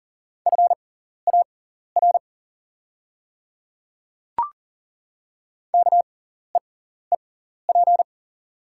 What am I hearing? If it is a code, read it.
Morse code, heard as a keyed tone, about 40 wpm, spaced as for 14 wpm, 700 Hz. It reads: FAR KEEP
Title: 6x Character Spacing - Top 500 Words - 40wpm